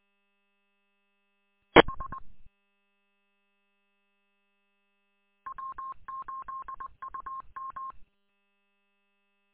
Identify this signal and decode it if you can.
Morse code, heard as a keyed tone, about 20 wpm, 1070 Hz. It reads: H W8UM
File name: w8um- -combined